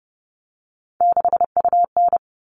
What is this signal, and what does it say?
Morse code, heard as a keyed tone, about 30 wpm, 700 Hz. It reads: 6UN
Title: All Cut Numbers - 3-Digit 30wpm